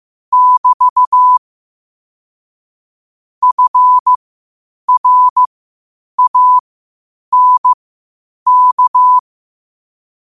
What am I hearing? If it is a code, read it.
Morse code, heard as a keyed tone, about 15 wpm, spaced as for 8 wpm, 1000 Hz. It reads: <BT> FRANK
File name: qotdfiles / 8wpm